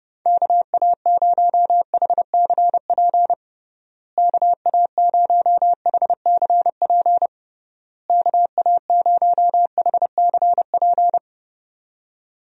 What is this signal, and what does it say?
Morse code, heard as a keyed tone, about 32 wpm, 700 Hz. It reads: KA0HCP KA0HCP KA0HCP
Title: USA Call Signs - 6 Characters - Repeated 3x - 30wpm